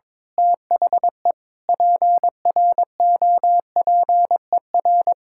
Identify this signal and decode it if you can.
Morse code, heard as a keyed tone, about 22 wpm, 700 Hz. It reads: THE PROPER
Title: Sentences from Top 1000 Words - Review Part 5 - 22wpm